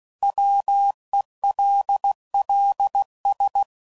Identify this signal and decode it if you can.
Morse code, heard as a keyed tone, about 16 wpm, 800 Hz. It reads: WELLS
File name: CW Code Groups @15WPM